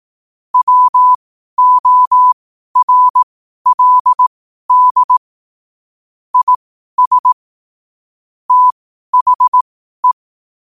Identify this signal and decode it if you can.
Morse code, heard as a keyed tone, about 18 wpm, spaced as for 12 wpm, 1000 Hz. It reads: WORLD IS THE
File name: qotdfiles / 12wpm